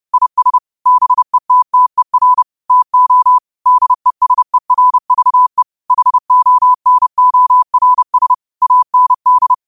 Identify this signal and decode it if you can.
Morse code, heard as a keyed tone, about 30 wpm, 1000 Hz. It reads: IS BETTER TO DESERVE HONORS AND
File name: 30wpm